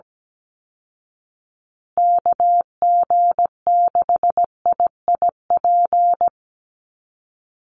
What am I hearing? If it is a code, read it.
Morse code, heard as a keyed tone, about 17 wpm, 700 Hz. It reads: KG6IIP
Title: Call Signs Repeated 2x - 17wpm